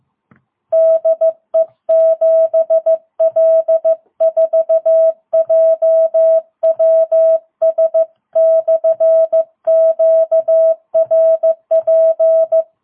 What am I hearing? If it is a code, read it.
Morse code, heard as a keyed tone, about 14 wpm, 660 Hz. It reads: DE 7L4JWS/QRP
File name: jh8cbh / 2022cw